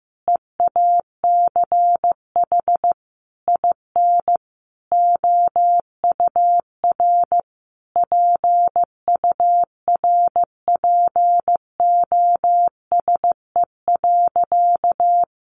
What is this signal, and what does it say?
Morse code, heard as a keyed tone, about 15 wpm, 700 Hz. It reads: EACH IN OUR PURPOSE.
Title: Sentences from Top 800 Words 15wpm